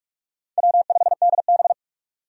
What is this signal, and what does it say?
Morse code, heard as a keyed tone, about 44 wpm, 700 Hz. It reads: W5DB
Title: USA Call Signs - 4 Characters - 45wpm